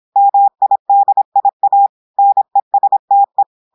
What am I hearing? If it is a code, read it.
Morse code, heard as a keyed tone, about 26 wpm, 800 Hz. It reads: MIDIA NESTE
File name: audios / portugais / portugais texte en clair 1 vitesse 26 mots minutes